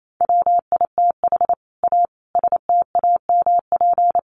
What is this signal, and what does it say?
Morse code, heard as a keyed tone, about 27 wpm, 700 Hz. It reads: WITH A STAMP
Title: A Sound Of Thunder 28wpm